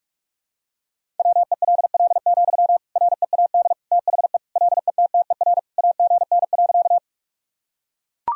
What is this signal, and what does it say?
Morse code, heard as a keyed tone, about 41 wpm, 700 Hz. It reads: WELL, READ THE LETTER AGN.
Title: Sentences from Top 300 Words and Abbreviations 45wpm